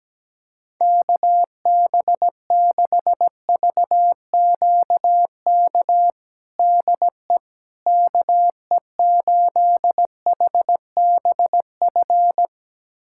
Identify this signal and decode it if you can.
Morse code, heard as a keyed tone, about 17 wpm, 700 Hz. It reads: KB6VQK DE KE8HBF
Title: USA Callsign DE Callsign - 6 Characters - 17wpm